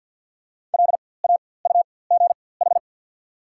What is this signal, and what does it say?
Morse code, heard as a keyed tone, about 48 wpm, spaced as for 23 wpm, 690 Hz. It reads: LAUGH
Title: 4x Character Spacing - Top 500 Words - 50wpm